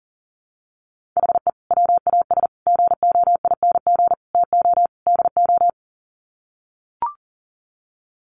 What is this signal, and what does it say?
Morse code, heard as a keyed tone, about 40 wpm, 700 Hz. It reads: HE WAS GOING TO DO
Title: Sets of 5 Words Encore2 40wpm